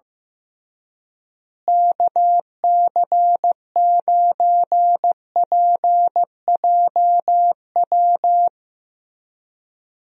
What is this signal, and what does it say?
Morse code, heard as a keyed tone, about 15 wpm, 700 Hz. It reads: KC9PJW
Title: USA Call Signs - 6 Characters - Repeated 3x - 15wpm